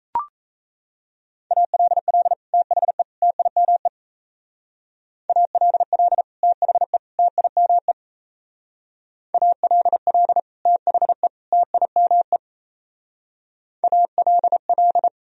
Morse code, heard as a keyed tone, about 34 wpm, 700 Hz. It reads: ALL THE TIME ALL THE TIME ALL THE TIME ALL
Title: Speed-Racing - Sets of 3 Words 28wpm